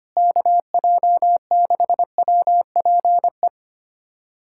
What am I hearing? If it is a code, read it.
Morse code, heard as a keyed tone, about 25 wpm, 700 Hz. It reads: KJ6WPE
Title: USA Callsign DE Callsign 25wpm